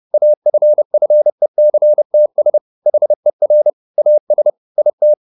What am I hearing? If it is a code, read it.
Morse code, heard as a keyed tone, about 30 wpm, 600 Hz. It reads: AFFECTS HER AS IT
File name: the old man and the sea / the old man and the sea 1 30wpm